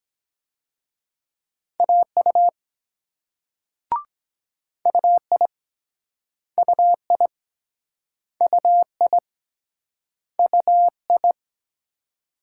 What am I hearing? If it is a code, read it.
Morse code, heard as a keyed tone, about 24 wpm, 700 Hz. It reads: AU UI UI UI UI